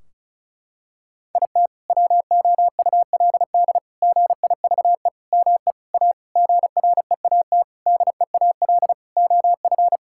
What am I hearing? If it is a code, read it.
Morse code, heard as a keyed tone, about 34 wpm, 700 Hz. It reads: IT WOULD GIVE ME A GREAT DEAL OF
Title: Sentences from Top 1600 Words 35wpm